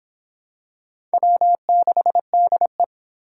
Morse code, heard as a keyed tone, about 25 wpm, 700 Hz. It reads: W6DE